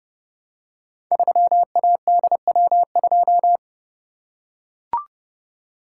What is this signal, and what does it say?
Morse code, heard as a keyed tone, about 29 wpm, 700 Hz. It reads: 3ADW2
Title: Code Groups - 5 Letter:Numbers - 30wpm